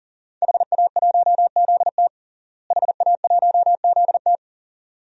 Morse code, heard as a keyed tone, about 41 wpm, 700 Hz. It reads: HA1ZT HA1ZT
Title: DXCC-100 Call Signs - 5 Characters - Repeated 2x - 40wpm